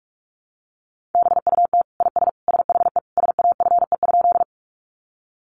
Morse code, heard as a keyed tone, about 44 wpm, 700 Hz. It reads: BUT IS SHE SAFE?
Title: Sentences from Top 900 Words 45wpm